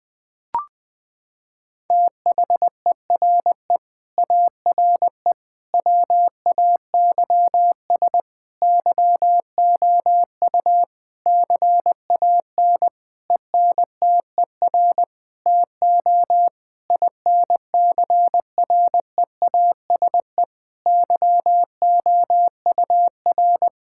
Morse code, heard as a keyed tone, about 20 wpm, 700 Hz. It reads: THERE ARE WAYS YOU CAN ENTER TO INCREASE YOUR
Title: Sentences with 8-Word Phrases - 20wpm